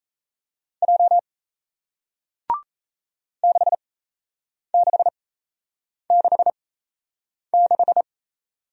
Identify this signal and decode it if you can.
Morse code, heard as a keyed tone, about 36 wpm, spaced as for 8 wpm, 700 Hz. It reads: J 6666